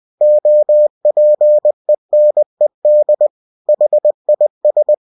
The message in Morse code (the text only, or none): OPENED HIS